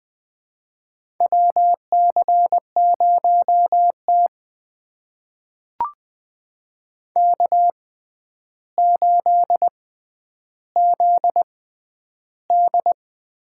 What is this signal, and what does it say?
Morse code, heard as a keyed tone, about 20 wpm, spaced as for 10 wpm, 700 Hz. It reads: WC0T K8ZD
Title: USA Call Signs - 4 Characters - 6x Character Spacing - 20wpm